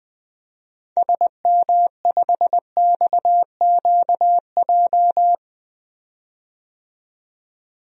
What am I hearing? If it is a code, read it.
Morse code, heard as a keyed tone, about 21 wpm, 700 Hz. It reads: SM5XQJ
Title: DXCC-100 Call Signs - 6 Characters - 20wpm